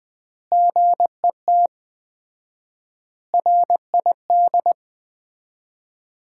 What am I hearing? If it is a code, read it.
Morse code, heard as a keyed tone, about 20 wpm, 700 Hz. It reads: GET RID
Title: 4x Word-Spacing - Sets of 4 Words - 20wpm